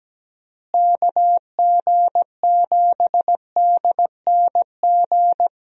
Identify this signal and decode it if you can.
Morse code, heard as a keyed tone, about 17 wpm, 700 Hz. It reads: KG7DNG
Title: USA Call Signs - 6 Characters - Repeated 2x - 17wpm